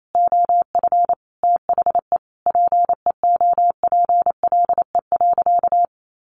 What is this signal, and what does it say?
Morse code, heard as a keyed tone, about 28 wpm, 700 Hz. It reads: OF THE PEOPLE.
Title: Sentences from Top 1500 Words 28wpm